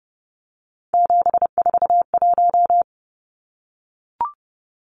Morse code, heard as a keyed tone, about 30 wpm, 700 Hz. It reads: 741
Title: Numbers - 3-Digit 30wpm